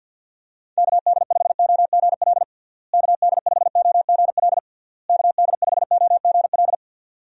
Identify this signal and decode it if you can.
Morse code, heard as a keyed tone, about 51 wpm, 700 Hz. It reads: KD5OGL KD5OGL KD5OGL
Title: USA Call Signs - 6 Characters - Repeated 3x - 50wpm